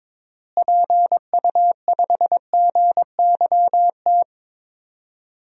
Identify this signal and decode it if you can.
Morse code, heard as a keyed tone, about 22 wpm, 700 Hz. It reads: PU5GYT
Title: DXCC-100 Call Signs - 6 Characters - Repeated 2x - 22wpm